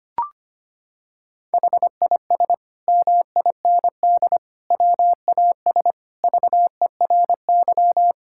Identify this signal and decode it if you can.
Morse code, heard as a keyed tone, about 25 wpm, 700 Hz. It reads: HIS MIND WAS VERY
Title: Sentences from Top 1000 Words - Review Part 6 - 25wpm